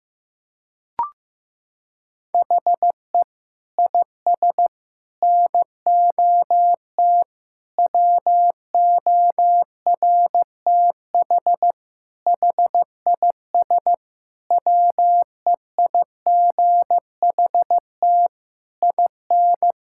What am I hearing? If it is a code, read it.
Morse code, heard as a keyed tone, about 16 wpm, 700 Hz. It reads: HE IS NOT WORTH HIS WEIGHT IN
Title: Sentences from Top 1000 Words - Review Part 5 - 15wpm